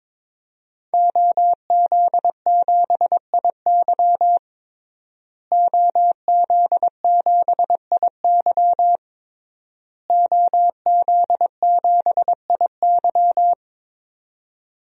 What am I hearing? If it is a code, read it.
Morse code, heard as a keyed tone, about 22 wpm, 700 Hz. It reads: OZ7IY OZ7IY OZ7IY
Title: DXCC-100 Call Signs - 5 Characters - Repeated 3x - 22wpm